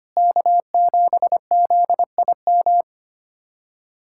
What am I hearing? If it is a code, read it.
Morse code, heard as a keyed tone, about 25 wpm, 700 Hz. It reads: K7ZIM
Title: USA Callsign DE Callsign - 5 Characters - 25wpm